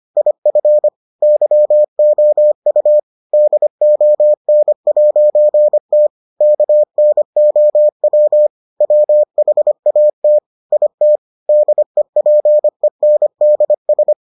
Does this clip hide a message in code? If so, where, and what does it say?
Yes: Morse code from 0.2 to 14.1 s, heard as a keyed tone, about 25 wpm, 600 Hz. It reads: IF YOU DON'T KNOW WHAT IT DEPENDS